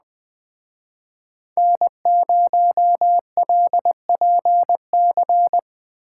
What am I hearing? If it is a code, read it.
Morse code, heard as a keyed tone, about 20 wpm, 700 Hz. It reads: N0LPC